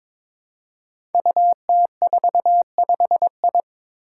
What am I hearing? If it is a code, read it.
Morse code, heard as a keyed tone, about 22 wpm, 700 Hz. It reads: UT45I
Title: Five Challenging Letter-Number Combinations 22wpm